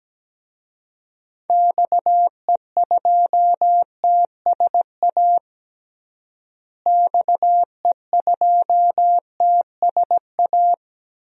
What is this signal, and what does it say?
Morse code, heard as a keyed tone, about 17 wpm, 700 Hz. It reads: XE2TSA XE2TSA
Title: DXCC-100 Call Signs - 6 Characters - Repeated 2x - 17wpm